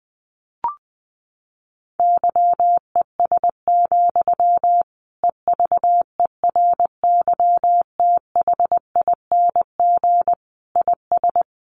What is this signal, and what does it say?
Morse code, heard as a keyed tone, about 20 wpm, 700 Hz. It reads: YES, EVERYTHING IS
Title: Sentences from Top 700 Words 20wpm